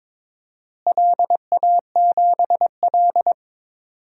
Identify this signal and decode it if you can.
Morse code, heard as a keyed tone, about 22 wpm, 700 Hz. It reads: LA7L
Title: DXCC-100 Call Signs - 4 Characters - Repeated 3x - 22wpm